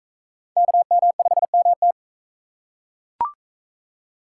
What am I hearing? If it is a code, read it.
Morse code, heard as a keyed tone, about 37 wpm, 700 Hz. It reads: KM5MT